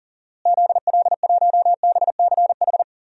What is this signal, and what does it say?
Morse code, heard as a keyed tone, about 40 wpm, 700 Hz. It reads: ZL1BCH
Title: DXCC-100 Call Signs - 6 Characters - Repeated 2x - 40wpm